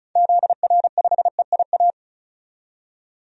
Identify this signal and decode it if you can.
Morse code, heard as a keyed tone, about 36 wpm, 700 Hz. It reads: ZR5EIA
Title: DXCC-100 Call Signs - 6 Characters - 35wpm